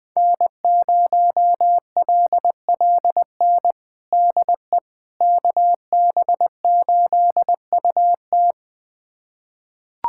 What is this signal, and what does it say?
Morse code, heard as a keyed tone, about 21 wpm, 700 Hz. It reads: N0LLN DE KB8UT
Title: USA Callsign DE Callsign - 5 Characters - 20wpm